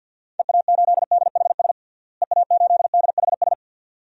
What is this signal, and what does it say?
Morse code, heard as a keyed tone, about 50 wpm, 700 Hz. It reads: EA8DHS EA8DHS
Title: DXCC-100 Call Signs - 6 Characters - Repeated 2x - 50wpm